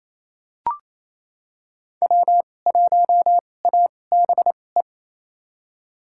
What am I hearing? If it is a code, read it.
Morse code, heard as a keyed tone, about 28 wpm, spaced as for 20 wpm, 700 Hz. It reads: W1ABE
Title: USA Call Signs - 5 Characters - 2x Character Spacing - 28wpm